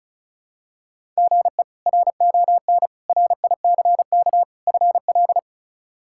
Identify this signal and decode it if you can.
Morse code, heard as a keyed tone, about 36 wpm, 700 Hz. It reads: GE RON RICK FL
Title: K1USN SST QSO Practice - 35wpm